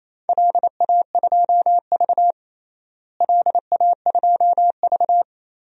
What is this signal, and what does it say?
Morse code, heard as a keyed tone, about 28 wpm, 700 Hz. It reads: LA2V LA2V
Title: DXCC-100 Call Signs - 4 Characters - Repeated 3x - 28wpm